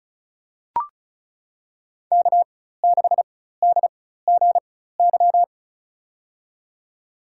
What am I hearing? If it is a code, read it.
Morse code, heard as a keyed tone, about 36 wpm, spaced as for 16 wpm, 700 Hz. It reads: K6DGY
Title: USA Call Signs - 5 Characters - 4x Character Spacing - 35wpm